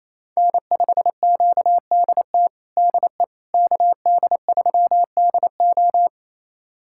Morse code, heard as a keyed tone, about 27 wpm, 700 Hz. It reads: N5QDT DE KD3DO
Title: USA Callsign DE Callsign - 5 Characters - 28wpm